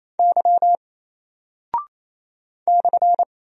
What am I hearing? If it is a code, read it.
Morse code, heard as a keyed tone, about 27 wpm, 700 Hz. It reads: Y /
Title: Single Character (Letter:Number:Punctuation:Prosign) 28wpm